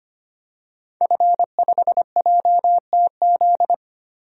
Morse code, heard as a keyed tone, about 25 wpm, 700 Hz. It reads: F5JTZ
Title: DXCC-100 Call Signs - 5 Characters - Repeated 2x - 25wpm